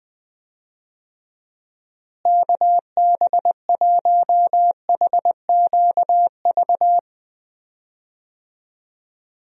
Morse code, heard as a keyed tone, about 19 wpm, 700 Hz. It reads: KB1HQV
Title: USA Call Signs - 6 Characters - Repeated 2x - 20wpm